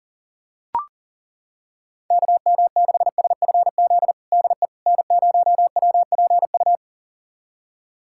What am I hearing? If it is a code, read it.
Morse code, heard as a keyed tone, about 40 wpm, 700 Hz. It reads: KM6SFZ DE N0WPU
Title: USA Callsign DE Callsign 40wpm